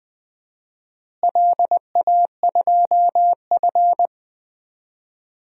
Morse code, heard as a keyed tone, about 20 wpm, 700 Hz. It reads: LA2F
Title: DXCC-100 Call Signs - 4 Characters - Repeated 2x - 20wpm